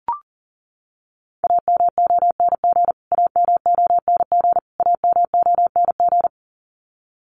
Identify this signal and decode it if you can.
Morse code, heard as a keyed tone, about 40 wpm, 700 Hz. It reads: AMONG AMONG AMONG
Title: Top 500 Words Repeated 40wpm